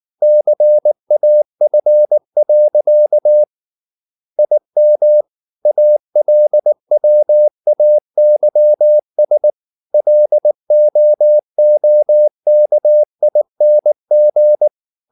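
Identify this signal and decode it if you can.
Morse code, heard as a keyed tone, about 19 wpm, 600 Hz. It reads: CAF. IM ALWAYS LOOKING
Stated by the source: Morse practice at 18 wpm, English text, from CW med Gnister.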